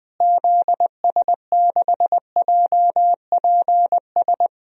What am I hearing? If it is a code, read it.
Morse code, heard as a keyed tone, about 21 wpm, 700 Hz. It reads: ZS6JPS
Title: DXCC-100 Call Signs - 6 Characters - Repeated 3x - 20wpm